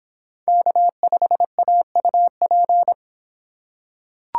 Morse code, heard as a keyed tone, about 26 wpm, 700 Hz. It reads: K5AUP